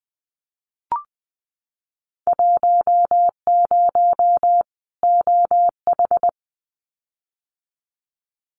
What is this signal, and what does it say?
Morse code, heard as a keyed tone, about 20 wpm, 700 Hz. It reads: 10 OH
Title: Winter Field Day Exchanges 20wpm